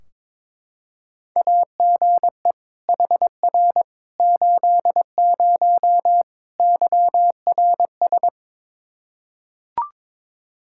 Morse code, heard as a keyed tone, about 22 wpm, 700 Hz. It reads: AGE HR 80 YRS
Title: K1USN SST QSO Practice 22wpm